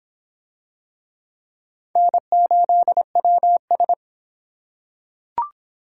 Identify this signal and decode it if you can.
Morse code, heard as a keyed tone, about 23 wpm, 700 Hz. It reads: N8WS